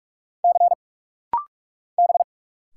Morse code, heard as a keyed tone, about 46 wpm, 700 Hz. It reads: C B
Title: Single Letters - Rapid-Fire 45wpm